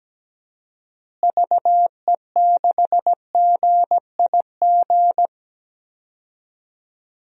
Morse code, heard as a keyed tone, about 17 wpm, 700 Hz. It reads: VE6GIG
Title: DXCC-100 Call Signs - 6 Characters - Repeated 3x - 17wpm